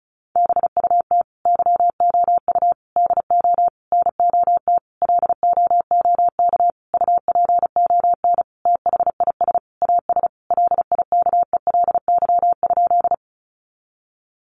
Morse code, heard as a keyed tone, about 34 wpm, 700 Hz. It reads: BUT YOU DO NOT LOOK UPON THIS AS LIKELY?
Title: Sentences from Top 1100 Words 35wpm